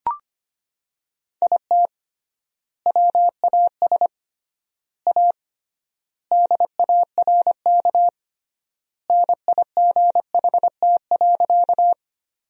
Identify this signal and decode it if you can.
Morse code, heard as a keyed tone, about 25 wpm, 700 Hz. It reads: IT WAS A DARK NIGHT.
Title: 3x Word-Spacing - Sentences from Top 500 Words - 25wpm